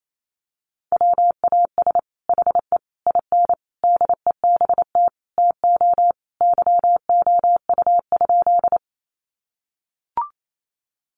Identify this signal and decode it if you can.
Morse code, heard as a keyed tone, about 28 wpm, 700 Hz. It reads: WAS HE IN DEBT TO YOU?
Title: Sentences from Top 1300 Words 28wpm